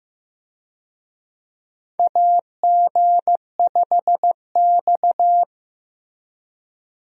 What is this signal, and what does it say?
Morse code, heard as a keyed tone, about 15 wpm, 700 Hz. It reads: AG5X
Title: USA Call Signs - 4 Characters - Repeated 3x - 15wpm